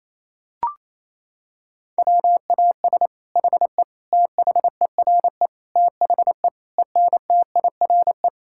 Morse code, heard as a keyed tone, about 28 wpm, 700 Hz. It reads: WAS HE THERE THE ENTIRE
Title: Sentences from Top 1000 Words - Review Part 5 - 28wpm